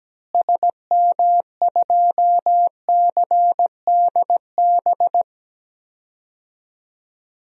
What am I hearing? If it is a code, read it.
Morse code, heard as a keyed tone, about 17 wpm, 700 Hz. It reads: SM2CDB
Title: DXCC-100 Call Signs - 6 Characters - Repeated 2x - 17wpm